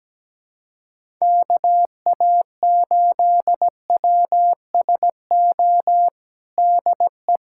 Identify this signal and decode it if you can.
Morse code, heard as a keyed tone, about 17 wpm, 700 Hz. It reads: KA8WSO DE
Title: USA Callsign DE Callsign - 6 Characters - 17wpm